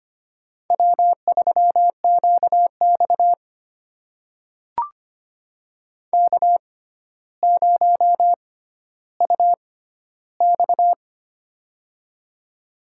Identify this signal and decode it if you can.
Morse code, heard as a keyed tone, about 25 wpm, spaced as for 13 wpm, 700 Hz. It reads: W3QX K0UX
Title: USA Call Signs - 6x Character Spacing - 25wpm